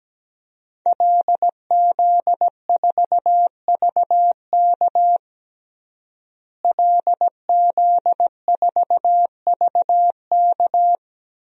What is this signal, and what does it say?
Morse code, heard as a keyed tone, about 17 wpm, 700 Hz. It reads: LZ4VK LZ4VK
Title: DXCC-100 Call Signs - 5 Characters - Repeated 3x - 17wpm